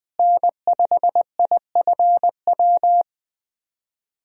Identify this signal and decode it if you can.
Morse code, heard as a keyed tone, about 21 wpm, 700 Hz. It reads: N5IFW
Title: USA Call Signs - 5 Characters - Repeated 2x - 20wpm